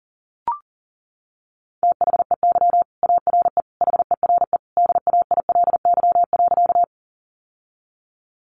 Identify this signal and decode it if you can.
Morse code, heard as a keyed tone, about 40 wpm, 700 Hz. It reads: THEY ARE HERE DAILY.
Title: Sentences from Top 1100 Words 40wpm